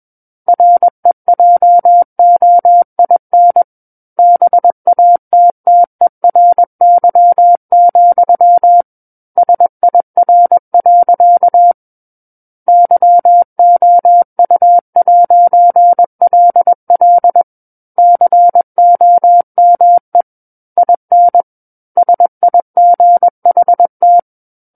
Morse code, heard as a keyed tone, about 21 wpm, 700 Hz. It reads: REJOIN BATTERY, SIR. YOU'LL COME IN SIGHT
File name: Book One / 12-Chapter 12 - 21 WPM